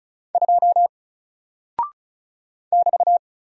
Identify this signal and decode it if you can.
Morse code, heard as a keyed tone, about 36 wpm, 700 Hz. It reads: 2 <BT>
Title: Single Character (Letter:Number:Punctuation:Prosign) 35wpm